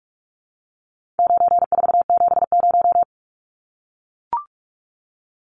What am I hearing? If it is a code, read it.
Morse code, heard as a keyed tone, about 44 wpm, 700 Hz. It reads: 9470
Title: Numbers - 4-Digit 45wpm